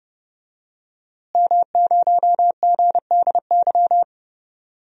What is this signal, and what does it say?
Morse code, heard as a keyed tone, about 30 wpm, 700 Hz. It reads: M0GDY
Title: DXCC-100 Call Signs - 5 Characters - Repeated 3x - 30wpm